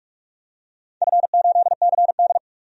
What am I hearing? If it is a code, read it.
Morse code, heard as a keyed tone, about 47 wpm, 700 Hz. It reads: F8CD
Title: DXCC-100 Call Signs - 4 Characters - 45wpm